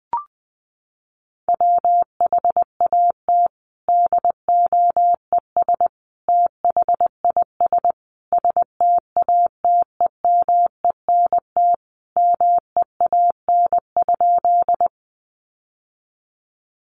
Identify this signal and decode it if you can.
Morse code, heard as a keyed tone, about 20 wpm, 700 Hz. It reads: WHAT DOES THIS STATEMENT MEAN?
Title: Sentences from Top 1200 Words 20wpm